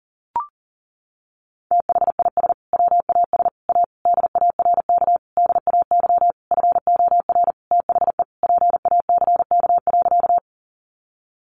Morse code, heard as a keyed tone, about 40 wpm, 700 Hz. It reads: THIS WAS A DARK DAY FOR THE PACK.
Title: Sentences from Top 1300 Words 40wpm